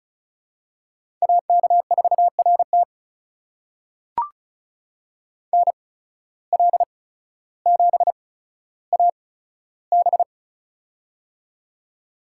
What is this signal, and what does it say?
Morse code, heard as a keyed tone, about 36 wpm, spaced as for 15 wpm, 690 Hz. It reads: AK4RT NL7AB
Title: USA Call Signs - 5 Characters - 8x Character Spacing - 35wpm